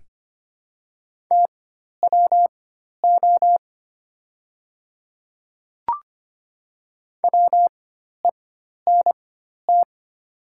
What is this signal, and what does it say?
Morse code, heard as a keyed tone, about 25 wpm, spaced as for 12 wpm, 700 Hz. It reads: TWO WENT
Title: 4x Character Spacing - Top 300 Words - 25wpm